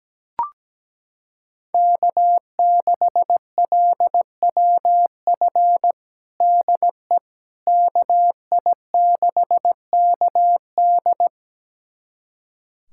Morse code, heard as a keyed tone, about 17 wpm, 700 Hz. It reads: K6LWF DE KI6KD
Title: USA Callsign DE Callsign - 5 Characters - 17wpm